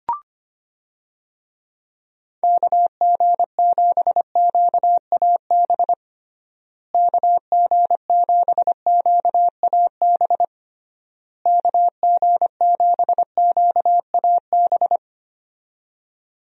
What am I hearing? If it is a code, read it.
Morse code, heard as a keyed tone, about 25 wpm, 700 Hz. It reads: KG7QAB KG7QAB KG7QAB
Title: USA Call Signs - 6 Characters - Repeated 3x - 25wpm